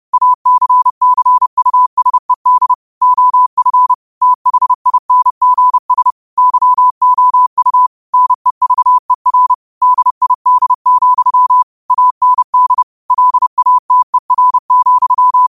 Morse code, heard as a keyed tone, about 30 wpm, 1000 Hz. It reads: ACCUSED OF THINGS YOU NEVER DID, AND LATER,